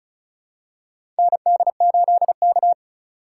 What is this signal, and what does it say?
Morse code, heard as a keyed tone, about 36 wpm, 700 Hz. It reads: ND8K
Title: USA Call Signs - 4x Character Spacing - 35wpm